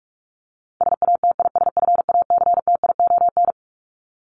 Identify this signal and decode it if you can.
Morse code, heard as a keyed tone, about 44 wpm, 700 Hz. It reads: SATISFACTION
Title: Top 1751-2000 Words 45wpm